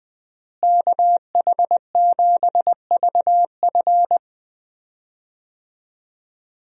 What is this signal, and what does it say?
Morse code, heard as a keyed tone, about 20 wpm, 700 Hz. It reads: KH7VF
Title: USA Call Signs - 5 Characters - Repeated 2x - 20wpm